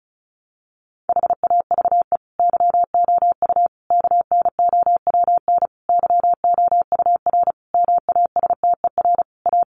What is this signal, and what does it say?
Morse code, heard as a keyed tone, about 34 wpm, 700 Hz. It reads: HAVE YOU KNOWN YOUR MASTER A
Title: Sentences from Top 1300 Words 35wpm